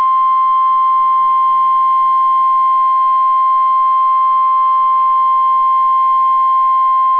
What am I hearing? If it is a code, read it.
Emergency Alert System (SAME) tones (the 1050 Hz attention tone). It is heard from 1.1 to 6.4 s.